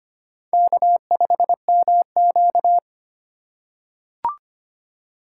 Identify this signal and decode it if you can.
Morse code, heard as a keyed tone, about 25 wpm, 700 Hz. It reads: K5MQ
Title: USA Call Signs - 4x Character Spacing - 25wpm